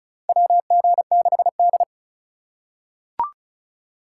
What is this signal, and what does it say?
Morse code, heard as a keyed tone, about 34 wpm, 700 Hz. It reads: WG6D
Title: USA Call Signs - 2x Character Spacing - 35wpm